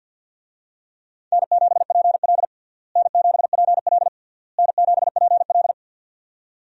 Morse code, heard as a keyed tone, about 51 wpm, 700 Hz. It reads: N7PL N7PL N7PL
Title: USA Call Signs - 4 Characters - Repeated 3x - 50wpm